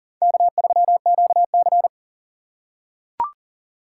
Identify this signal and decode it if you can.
Morse code, heard as a keyed tone, about 40 wpm, 700 Hz. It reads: K3QC
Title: USA Call Signs - 4 Characters - 6x Character Spacing - 40wpm